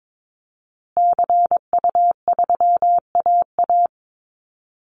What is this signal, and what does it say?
Morse code, heard as a keyed tone, about 22 wpm, 700 Hz. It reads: CU3AA
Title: Call Signs Repeated 3x - 22wpm